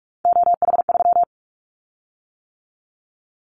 Morse code, heard as a keyed tone, about 46 wpm, 700 Hz. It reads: OH3
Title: Callsign Prefixes for Top 25 DXCC - 45wpm